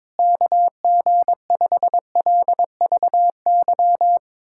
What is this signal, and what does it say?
Morse code, heard as a keyed tone, about 22 wpm, 700 Hz. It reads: KG5LVY
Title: USA Callsign DE Callsign - 6 Characters - 22wpm